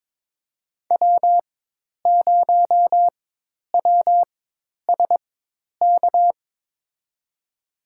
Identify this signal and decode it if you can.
Morse code, heard as a keyed tone, about 22 wpm, spaced as for 10 wpm, 700 Hz. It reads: W0WSK
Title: USA Call Signs - 5 Characters - 4x Character Spacing - 22wpm